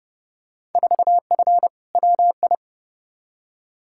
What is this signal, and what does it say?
Morse code, heard as a keyed tone, about 31 wpm, 700 Hz. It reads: 4F WI
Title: Field Day Exchanges - 30wpm